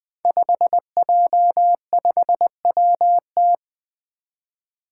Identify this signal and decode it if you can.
Morse code, heard as a keyed tone, about 20 wpm, 700 Hz. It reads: 5J5WT
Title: Five Challenging Letter-Number Combinations 20wpm